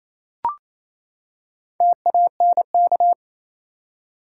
Morse code, heard as a keyed tone, about 28 wpm, 700 Hz. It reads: TANK
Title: Four Letter Words - 28wpm